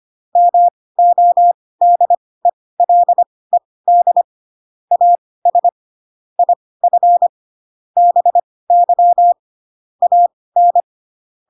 Morse code, heard as a keyed tone, about 25 wpm, spaced as for 18 wpm, 700 Hz. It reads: MODELED AS IF BY AN